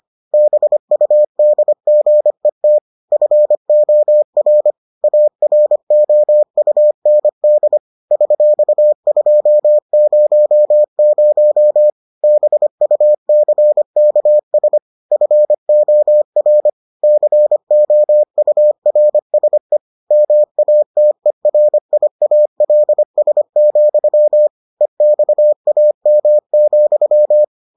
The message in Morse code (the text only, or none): BUDGET FOR AROUND $200 BUCKS FOR COURSE MATERIALS, EXAM,